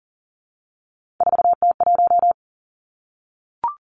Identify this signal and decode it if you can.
Morse code, heard as a keyed tone, about 40 wpm, 700 Hz. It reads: 4T1
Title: Three Challenging Letter-Number Combinations 40wpm